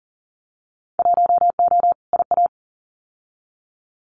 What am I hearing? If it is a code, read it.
Morse code, heard as a keyed tone, about 40 wpm, 700 Hz. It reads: 1O IA
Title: Winter Field Day Exchanges Encore 40wpm